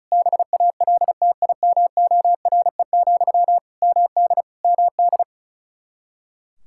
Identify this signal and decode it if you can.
Morse code, heard as a keyed tone, about 34 wpm, 700 Hz. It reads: BALTIMORE, MD MD
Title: Practice QSO - Exchange 1 - Beginning - 35wpm